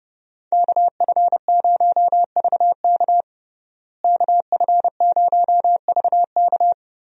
Morse code, heard as a keyed tone, about 32 wpm, 700 Hz. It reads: KF0VK KF0VK
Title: USA Call Signs - 5 Characters - Repeated 3x - 30wpm